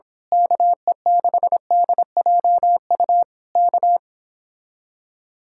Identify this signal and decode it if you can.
Morse code, heard as a keyed tone, about 26 wpm, 700 Hz. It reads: KE6DJU K
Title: Speed-Racing Common QSO Phrases - 22wpm